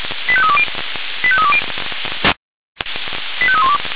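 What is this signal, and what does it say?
Five-tone selective call (ZVEI), heard as quick runs of five tones, tones of about 80 ms. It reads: 85320 85320 85312
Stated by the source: Received by SDR, ZVEI five-tone selective call.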